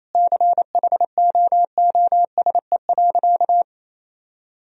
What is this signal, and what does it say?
Morse code, heard as a keyed tone, about 28 wpm, 700 Hz. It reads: CHOOSE.
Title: Sentences from Top 1000 Words - Review Part 4 - 28wpm